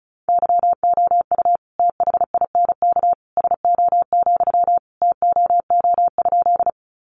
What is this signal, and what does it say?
Morse code, heard as a keyed tone, about 34 wpm, 700 Hz. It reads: YOU THINK SO, TOO?
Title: Sentences from Top 200 Words 35wpm